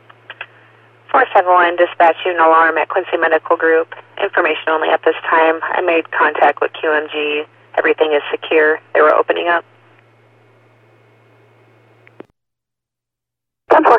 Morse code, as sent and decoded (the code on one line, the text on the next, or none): none